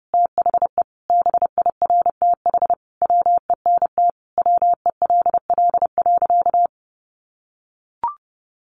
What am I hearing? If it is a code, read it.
Morse code, heard as a keyed tone, about 30 wpm, 700 Hz. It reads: THE BIRTH WENT WELL.
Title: Sentences from Top 800 Words 30wpm